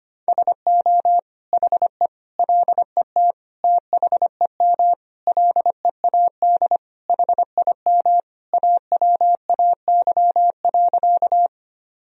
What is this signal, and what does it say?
Morse code, heard as a keyed tone, about 25 wpm, 700 Hz. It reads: SO HE LET THEM LEAD HIM AWAY.
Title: Sentences from Top 1000 Words - Review Part 2 - 25wpm